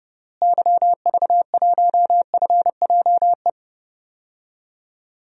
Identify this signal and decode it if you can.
Morse code, heard as a keyed tone, about 30 wpm, 700 Hz. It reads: YV1FJE
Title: DXCC-100 Call Signs - 6 Characters - Repeated 3x - 30wpm